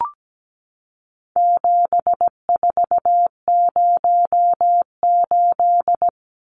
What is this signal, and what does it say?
Morse code, heard as a keyed tone, about 17 wpm, 700 Hz. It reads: 7408
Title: Numbers - 4-Digit 17wpm